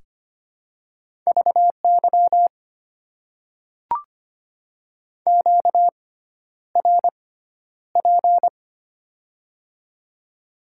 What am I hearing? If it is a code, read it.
Morse code, heard as a keyed tone, about 25 wpm, spaced as for 11 wpm, 700 Hz. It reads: VY QRP
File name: Top 90 QSO Elements - 6x Character Spacing - 25wpm